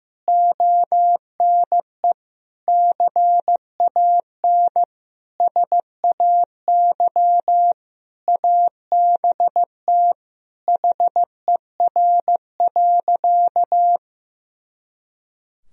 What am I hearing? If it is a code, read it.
Morse code, heard as a keyed tone, about 15 wpm, 700 Hz. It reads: ONE CAN SAY ABT HER.
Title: Sentences from Top 200 Words and Abbreviations 15wpm